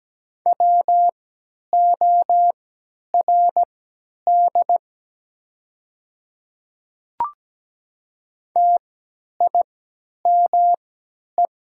Morse code, heard as a keyed tone, about 17 wpm, spaced as for 10 wpm, 700 Hz. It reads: WORD TIME
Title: 3x Character Spacing - Top 100 Words - 17wpm